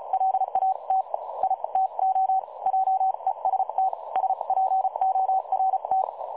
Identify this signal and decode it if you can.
Morse code, heard as a keyed tone, about 36 wpm, spaced as for 31 wpm, 760 Hz. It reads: LAT I TO JEST SPORT
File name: dlazaawansowanych / nagraniazpasma / websdr recording 3565.0kHz